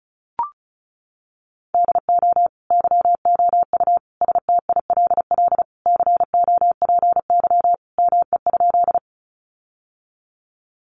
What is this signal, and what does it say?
Morse code, heard as a keyed tone, about 34 wpm, 700 Hz. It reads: DO YOU STILL COPY ME?
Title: Sentences from Top 1400 Words 35wpm